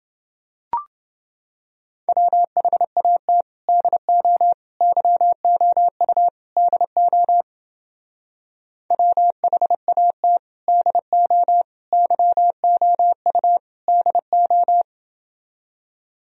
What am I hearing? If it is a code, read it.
Morse code, heard as a keyed tone, about 29 wpm, 700 Hz. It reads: WHAT DO YOU DO WHAT DO YOU DO